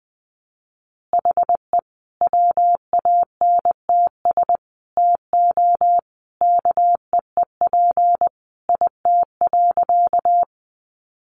Morse code, heard as a keyed tone, about 19 wpm, 700 Hz. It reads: HE WANTS TO KEEP IT.
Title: Additional Easy Phrases 20wpm